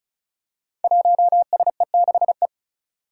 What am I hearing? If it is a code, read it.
Morse code, heard as a keyed tone, about 36 wpm, 700 Hz. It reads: 1SE6E
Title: Five Challenging Letter-Number Combinations 35wpm